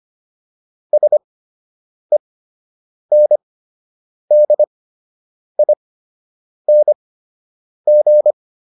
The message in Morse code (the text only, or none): SENDING